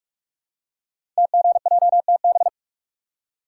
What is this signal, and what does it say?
Morse code, heard as a keyed tone, about 44 wpm, 700 Hz. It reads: TGJTB
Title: Code Groups - 5 Letters - 45wpm